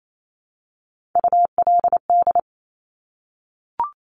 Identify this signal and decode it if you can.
Morse code, heard as a keyed tone, about 28 wpm, 700 Hz. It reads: ULD
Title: Top Three Letter Combinations in Words 28wpm